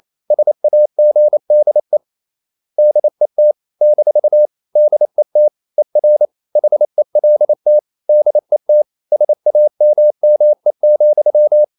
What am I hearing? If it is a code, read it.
Morse code, heard as a keyed tone, about 28 wpm, 600 Hz. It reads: SAGDE DET - DET ER HELT DET SAMME,
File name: Ferie QRQ 28wpm